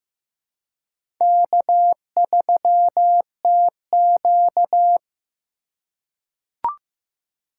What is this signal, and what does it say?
Morse code, heard as a keyed tone, about 15 wpm, 700 Hz. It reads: K3TQ
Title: USA Call Signs - 4 Characters - 2x Character Spacing - 15wpm